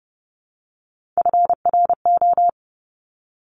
Morse code, heard as a keyed tone, about 30 wpm, 690 Hz. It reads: FRO
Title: Top Three Letter Combinations in Words 30wpm